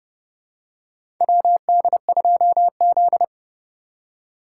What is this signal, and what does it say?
Morse code, heard as a keyed tone, about 30 wpm, 700 Hz. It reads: WD2Z
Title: USA Call Signs - 4 Characters - 4x Character Spacing - 30wpm